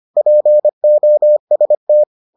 Morse code, heard as a keyed tone, about 25 wpm, 600 Hz. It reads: POST